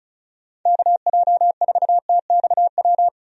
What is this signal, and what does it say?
Morse code, heard as a keyed tone, about 36 wpm, 700 Hz. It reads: KJ4TXW
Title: USA Call Signs - 6 Characters - 35wpm